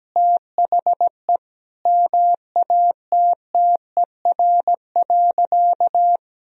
Morse code, heard as a keyed tone, about 17 wpm, 700 Hz. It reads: THE MATTER.
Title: Sentences from Top 1000 Words - Review Part 4 - 17wpm